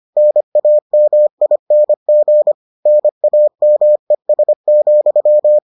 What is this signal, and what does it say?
Morse code, heard as a keyed tone, about 25 wpm, 600 Hz. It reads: NAMING NAMES,